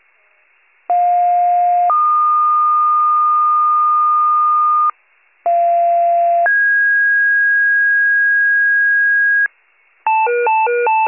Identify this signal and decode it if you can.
Two-tone page, heard as a one-second tone, then a three-second tone, about 710 then 1220 Hz, about 690 then 1650 Hz, with noise. It is heard from 0.9 to 9.5 s.
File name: pager / 41 RESCUE DUTY / wcalert